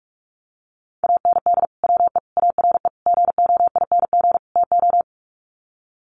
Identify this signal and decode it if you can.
Morse code, heard as a keyed tone, about 44 wpm, 700 Hz. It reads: AND WE ARE GOING TO
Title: Sets of 5 Words Encore2 45wpm